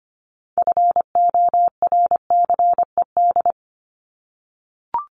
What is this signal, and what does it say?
Morse code, heard as a keyed tone, about 25 wpm, 700 Hz. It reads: FORCED
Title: ED Suffix - Top 100 Words with Base Words 25wpm